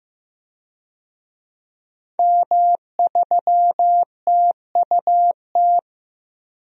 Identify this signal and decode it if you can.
Morse code, heard as a keyed tone, about 15 wpm, 700 Hz. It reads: M3TUT
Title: DXCC-100 Call Signs - 5 Characters - Repeated 2x - 15wpm